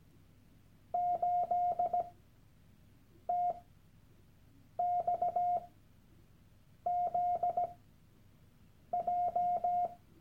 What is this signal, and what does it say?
Morse code, heard as a keyed tone, about 17 wpm, spaced as for 6 wpm, 710 Hz. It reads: 8TXZJ
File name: Workshop Dateien / CW-Online-Kurs / Lektion 001